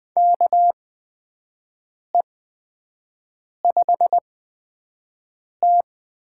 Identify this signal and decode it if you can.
Morse code, heard as a keyed tone, about 20 wpm, spaced as for 5 wpm, 700 Hz. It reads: KE5T
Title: USA Call Signs - 6 Characters - 8x Character Spacing - 20wpm